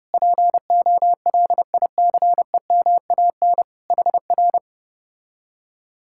Morse code, heard as a keyed tone, about 33 wpm, 700 Hz. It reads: POLICEMAN HR